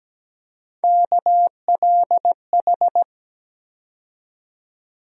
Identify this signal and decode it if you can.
Morse code, heard as a keyed tone, about 17 wpm, 700 Hz. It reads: KLH
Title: USA Call Sign Postfixes - 3 Letters 17wpm